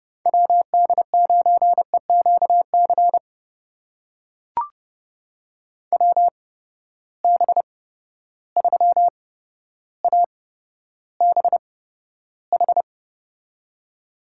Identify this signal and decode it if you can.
Morse code, heard as a keyed tone, about 30 wpm, spaced as for 13 wpm, 700 Hz. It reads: WD9EQC WB3ABH
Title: USA Call Signs - 6 Characters - 8x Character Spacing - 30wpm